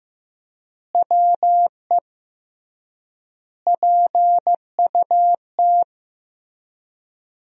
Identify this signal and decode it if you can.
Morse code, heard as a keyed tone, about 15 wpm, 700 Hz. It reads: WE PUT
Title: 3x Word-Spacing - Sentences from Top 200 Words - 15wpm